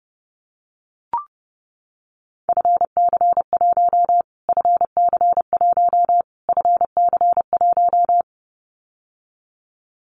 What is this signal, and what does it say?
Morse code, heard as a keyed tone, about 30 wpm, 700 Hz. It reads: FC1 FC1 FC1
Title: Callsign Prefixes for Top 25 DXCC - Repeated 3x - 30wpm